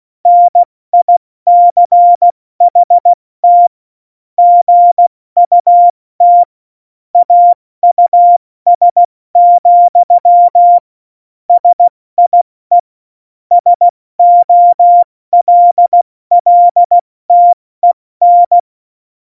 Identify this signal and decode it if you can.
Morse code, heard as a keyed tone, about 16 wpm, 700 Hz. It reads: NICHT GUT AUS, SIE SOLLTEN
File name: Morsekurs / Witze / Witz 01. Oktober 2019, 70 F80